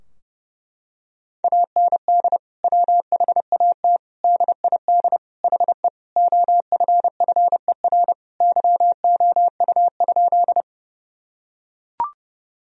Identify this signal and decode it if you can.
Morse code, heard as a keyed tone, about 31 wpm, 700 Hz. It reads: AND WHAT DID HE OFFER YOU?
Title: Sentences from Top 1000 Words - Review Part 4 - 30wpm